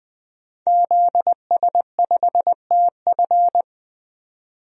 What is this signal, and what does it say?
Morse code, heard as a keyed tone, about 20 wpm, 700 Hz. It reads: ZS5TF
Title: DXCC-100 Call Signs - 5 Characters - 20wpm